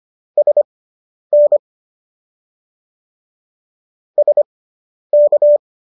Morse code, heard as a keyed tone, about 25 wpm, spaced as for 10 wpm, 590 Hz. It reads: SN SK